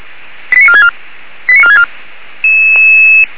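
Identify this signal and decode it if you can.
Five-tone selective call (ZVEI), heard as quick runs of five tones, tones of about 70 ms. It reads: 89464 89464
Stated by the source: Received on SDR.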